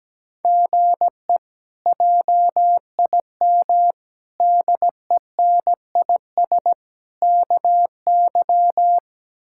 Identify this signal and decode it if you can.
Morse code, heard as a keyed tone, about 17 wpm, 700 Hz. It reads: GE JIM DENIS KY
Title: K1USN SST QSO Practice - 17wpm